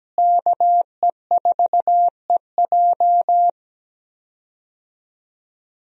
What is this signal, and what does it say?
Morse code, heard as a keyed tone, about 17 wpm, 700 Hz. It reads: KE4EJ
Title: USA Call Signs - 5 Characters - Repeated 3x - 17wpm